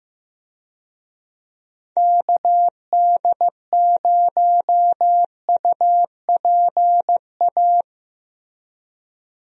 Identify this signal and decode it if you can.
Morse code, heard as a keyed tone, about 15 wpm, 700 Hz. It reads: KD0UPA
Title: USA Call Signs - 6 Characters - Repeated 2x - 15wpm